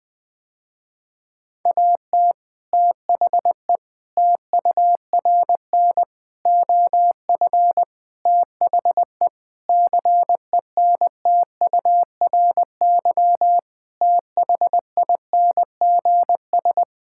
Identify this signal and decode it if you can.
Morse code, heard as a keyed tone, about 21 wpm, 700 Hz. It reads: AT THE TURN OF THE CENTURY THINGS
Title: Sentences with 5-Word Phrases - Spoken Beginning - 20wpm